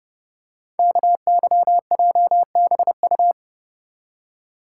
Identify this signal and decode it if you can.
Morse code, heard as a keyed tone, about 28 wpm, 700 Hz. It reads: KYJBU
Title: Code Groups - 5 Letters - 30wpm